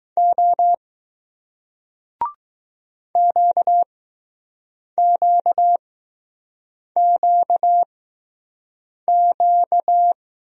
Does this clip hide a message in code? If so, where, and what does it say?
Yes: Morse code from 0.2 to 10.1 s, heard as a keyed tone, about 20 wpm, 700 Hz. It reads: O Q Q Q Q